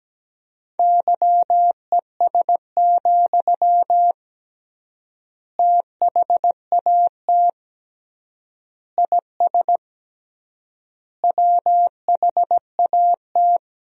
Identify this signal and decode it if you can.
Morse code, heard as a keyed tone, about 17 wpm, 700 Hz. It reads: YES, THAT IS WHAT
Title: 3x Word-Spacing - Sentences from Top 500 Words - 17wpm